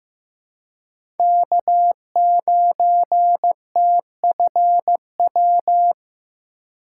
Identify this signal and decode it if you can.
Morse code, heard as a keyed tone, about 15 wpm, 700 Hz. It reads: K9TFW
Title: USA Call Signs - 5 Characters - 2x Character Spacing - 15wpm